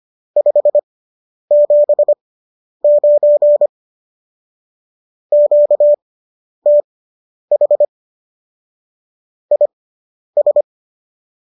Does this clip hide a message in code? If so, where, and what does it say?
Yes: Morse code from 0.4 to 10.6 s, heard as a keyed tone, about 25 wpm, spaced as for 10 wpm, 590 Hz. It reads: 579 QTH IS